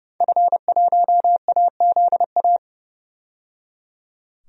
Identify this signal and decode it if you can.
Morse code, heard as a keyed tone, about 30 wpm, 700 Hz. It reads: F1AZA
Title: DXCC-100 Call Signs - 5 Characters - 30wpm